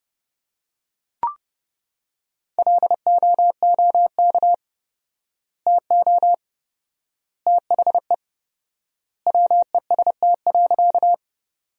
Morse code, heard as a keyed tone, about 32 wpm, 700 Hz. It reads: LOOK TO THE WEST.
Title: 4x Word-Spacing - Sentences from Top 400 Words - 30wpm